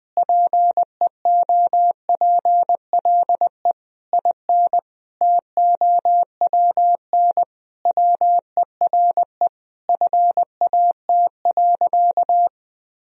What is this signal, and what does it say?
Morse code, heard as a keyed tone, about 20 wpm, 700 Hz. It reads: PEOPLE IN TOWN WERE FAT.
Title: Sentences from Top 1000 Words - Review Part 4 - 20wpm